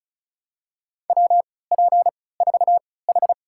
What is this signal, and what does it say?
Morse code, heard as a keyed tone, about 36 wpm, spaced as for 20 wpm, 700 Hz. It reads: WP4H
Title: USA Call Signs - 3x Character Spacing - 35wpm